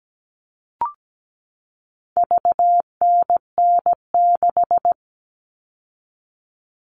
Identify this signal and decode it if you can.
Morse code, heard as a keyed tone, about 17 wpm, 700 Hz. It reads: VNN6
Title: All Cut Numbers - 4-Digit 17wpm